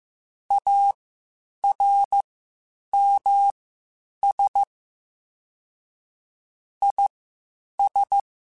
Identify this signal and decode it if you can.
Morse code, heard as a keyed tone, about 15 wpm, spaced as for 8 wpm, 800 Hz. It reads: ARMS IS
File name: Quotes Spaced / Quotes Spaced-mp3s / ChildAsleep sp @13